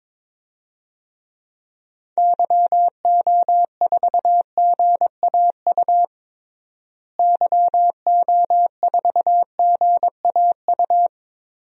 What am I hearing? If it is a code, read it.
Morse code, heard as a keyed tone, about 22 wpm, 700 Hz. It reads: YO4GAU YO4GAU
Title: DXCC-100 Call Signs - 6 Characters - Repeated 3x - 22wpm